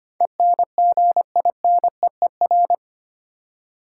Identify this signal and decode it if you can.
Morse code, heard as a keyed tone, about 25 wpm, 700 Hz. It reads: ENGINEER
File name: 8 Letter Words - First 3 Letters Given - 25wpm